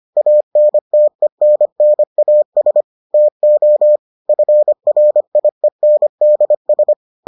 Morse code, heard as a keyed tone, about 25 wpm, 600 Hz. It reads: ANTENNAS TO FRIENDS